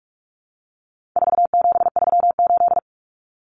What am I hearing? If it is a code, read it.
Morse code, heard as a keyed tone, about 44 wpm, 700 Hz. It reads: 4738
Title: Numbers - 4-Digit 45wpm